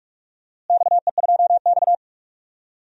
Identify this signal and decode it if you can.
Morse code, heard as a keyed tone, about 46 wpm, 700 Hz. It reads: XE2X
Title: DXCC-100 Call Signs - 4 Characters - Repeated 2x - 45wpm